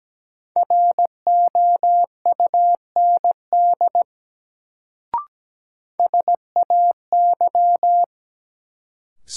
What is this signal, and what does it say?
Morse code, heard as a keyed tone, about 17 wpm, 700 Hz. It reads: ROUND SAY
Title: Sentences from Top 200 Words - New Word Drill - 17wpm